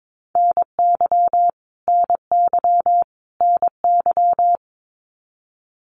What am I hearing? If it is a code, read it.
Morse code, heard as a keyed tone, about 22 wpm, 700 Hz. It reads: NY NY NY
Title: US States Repeated 3x 22wpm